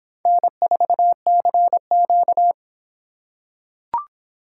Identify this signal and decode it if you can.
Morse code, heard as a keyed tone, about 25 wpm, 700 Hz. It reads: N4CQ